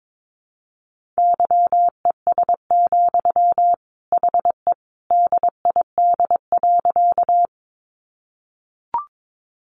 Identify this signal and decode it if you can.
Morse code, heard as a keyed tone, about 22 wpm, 700 Hz. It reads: YES, HE DID.
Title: Sentences from Top 500 Words 22wpm